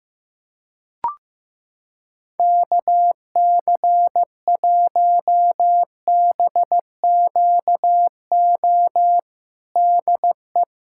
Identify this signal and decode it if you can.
Morse code, heard as a keyed tone, about 15 wpm, 700 Hz. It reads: KC1BQO DE
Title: USA Callsign DE Callsign - 6 Characters - 15wpm